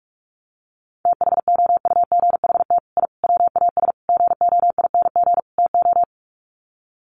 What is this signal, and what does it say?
Morse code, heard as a keyed tone, about 44 wpm, 700 Hz. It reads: THOUGHT I WAS GOING TO
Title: Sets of 5 Words Encore2 45wpm